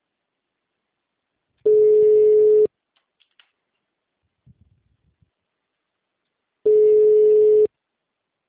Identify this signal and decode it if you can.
Ringback tone (Europe, 425 Hz).
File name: out- -2011- - - .13